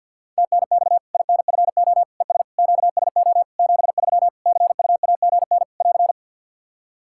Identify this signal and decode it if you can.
Morse code, heard as a keyed tone, about 53 wpm, 700 Hz. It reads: TNX INFO ES QSO 73 CUAGN <AR>
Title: K1USN SST QSO Practice 50wpm